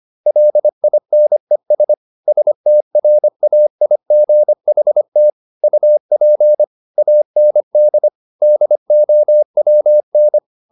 Morse code, heard as a keyed tone, about 25 wpm, 600 Hz. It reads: LINES STRAIGHT UP AND DOWN
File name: the old man and the sea part 1 25wpm